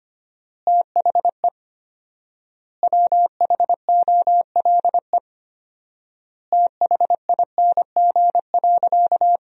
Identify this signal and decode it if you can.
Morse code, heard as a keyed tone, about 25 wpm, 700 Hz. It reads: THE WHOLE THING.
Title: 4x Word-Spacing - Sentences from Top 400 Words - 25wpm